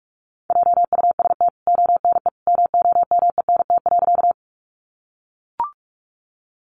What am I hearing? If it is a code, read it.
Morse code, heard as a keyed tone, about 44 wpm, 700 Hz. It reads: JUST ONE MOMENT.
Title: Sentences from Top 1000 Words 45wpm